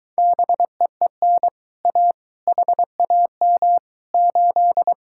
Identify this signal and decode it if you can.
Morse code, heard as a keyed tone, about 23 wpm, 700 Hz. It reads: BEEN A HAM 8